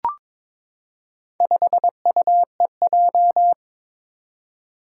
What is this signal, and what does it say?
Morse code, heard as a keyed tone, about 22 wpm, 700 Hz. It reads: 5UEJ
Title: Four Challenging Letter-Number Combinations 22wpm